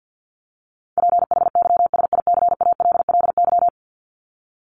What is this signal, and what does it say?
Morse code, heard as a keyed tone, about 50 wpm, 700 Hz. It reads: PHYSICALLY
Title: Top 1501-1750 Words 50wpm